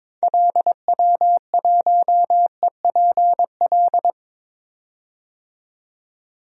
Morse code, heard as a keyed tone, about 22 wpm, 700 Hz. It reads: LW1EPL
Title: DXCC-100 Call Signs - 6 Characters - 22wpm